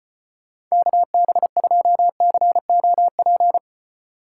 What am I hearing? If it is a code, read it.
Morse code, heard as a keyed tone, about 31 wpm, 700 Hz. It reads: KB2COP
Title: Speed Racing - USA 2x3 Call Signs 25wpm